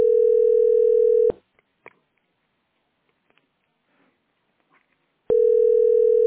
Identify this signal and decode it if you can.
Ringback tone (North America), 440 + 480 Hz.